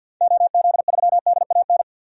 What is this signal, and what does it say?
Morse code, heard as a keyed tone, about 51 wpm, 700 Hz. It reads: OZ3DAN
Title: DXCC-100 Call Signs - 6 Characters - 50wpm